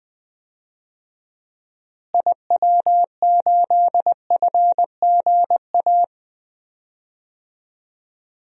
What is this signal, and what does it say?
Morse code, heard as a keyed tone, about 20 wpm, 700 Hz. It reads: IW8FGA
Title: DXCC-100 Call Signs - 6 Characters - Repeated 3x - 20wpm